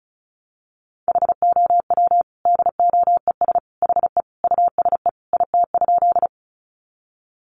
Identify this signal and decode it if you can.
Morse code, heard as a keyed tone, about 34 wpm, 700 Hz. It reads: HOW DOES HE USE IT?
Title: Sentences from Top 200 Words 35wpm